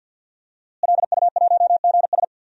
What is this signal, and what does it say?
Morse code, heard as a keyed tone, about 47 wpm, 700 Hz. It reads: LU1GS
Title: DXCC-100 Call Signs - 5 Characters - Repeated 3x - 50wpm